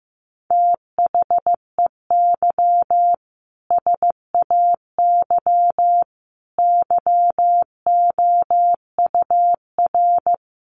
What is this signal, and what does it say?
Morse code, heard as a keyed tone, about 15 wpm, 700 Hz. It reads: THEY SAY YOUR
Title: Sentences from Top 1100 Words 15wpm